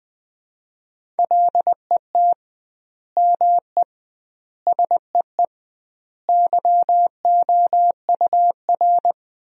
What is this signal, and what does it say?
Morse code, heard as a keyed tone, about 20 wpm, 700 Hz. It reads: LET ME SEE YOUR
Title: 2x Word-Spacing - Sentences from Top 500 Words - 20wpm